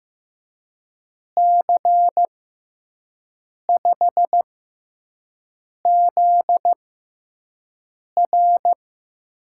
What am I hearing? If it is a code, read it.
Morse code, heard as a keyed tone, about 15 wpm, spaced as for 5 wpm, 700 Hz. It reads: C5ZR
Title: USA Call Signs - 5 Characters - 6x Character Spacing - 15wpm